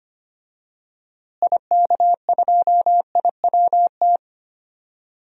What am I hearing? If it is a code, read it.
Morse code, heard as a keyed tone, about 25 wpm, 700 Hz. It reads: IK2IWT
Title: DXCC-100 Call Signs - 6 Characters - Repeated 3x - 25wpm